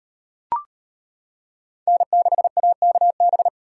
Morse code, heard as a keyed tone, about 38 wpm, 700 Hz. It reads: N6AKB